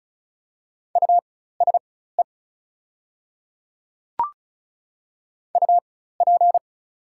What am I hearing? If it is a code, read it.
Morse code, heard as a keyed tone, about 35 wpm, spaced as for 16 wpm, 700 Hz. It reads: USE UP
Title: 4x Character Spacing - Top 100 Words - 35wpm